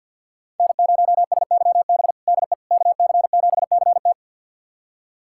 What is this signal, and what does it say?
Morse code, heard as a keyed tone, about 46 wpm, 700 Hz. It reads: N0SYB DE KC7CT
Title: USA Callsign DE Callsign - 5 Characters - 50wpm